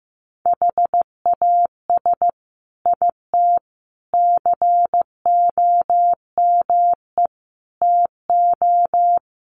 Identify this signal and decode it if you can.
Morse code, heard as a keyed tone, about 15 wpm, 700 Hz. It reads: HAS IT COME TO
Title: Sentences from Top 100 Words 15wpm